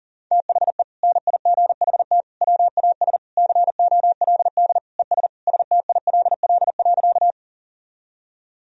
Morse code, heard as a keyed tone, about 37 wpm, 700 Hz. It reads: THE NIGHT WAS COLD ES STILL.
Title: Sentences from Top 400 Words and Abbreviations 40wpm